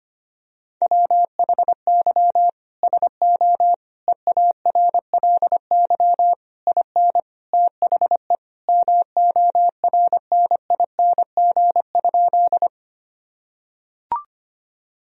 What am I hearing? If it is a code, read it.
Morse code, heard as a keyed tone, about 25 wpm, 700 Hz. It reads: WHY SO EARLY IN THE MORNING?
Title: Sentences from Top 1000 Words - Review Part 2 - 25wpm